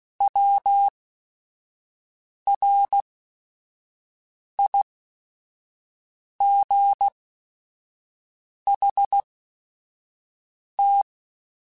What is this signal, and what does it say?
Morse code, heard as a keyed tone, about 16 wpm, spaced as for 5 wpm, 800 Hz. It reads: WRIGHT
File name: Short Runs / Short Runs-mp3s / Words @5